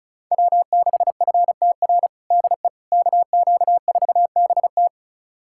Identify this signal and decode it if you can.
Morse code, heard as a keyed tone, about 36 wpm, 700 Hz. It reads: W6FTR DE KQ4BT
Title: USA Callsign DE Callsign - 5 Characters - 35wpm